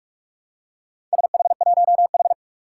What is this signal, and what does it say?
Morse code, heard as a keyed tone, about 46 wpm, 700 Hz. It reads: SH1H
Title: Four Challenging Letter-Number Combinations 45wpm